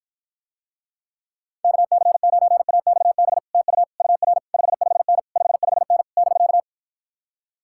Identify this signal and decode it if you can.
Morse code, heard as a keyed tone, about 53 wpm, 700 Hz. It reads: KC9AXB TU UR 55N 55N <BK>